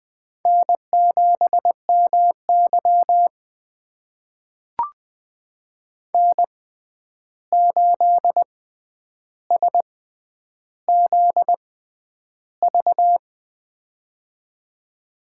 Morse code, heard as a keyed tone, about 20 wpm, spaced as for 10 wpm, 700 Hz. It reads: N7MY N8SZV
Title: USA Call Signs - 6x Character Spacing - 20wpm